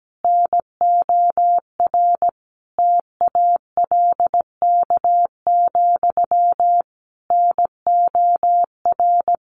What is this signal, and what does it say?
Morse code, heard as a keyed tone, about 17 wpm, 700 Hz. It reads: NOR TALK, NOR
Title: Sentences from Top 1500 Words 17wpm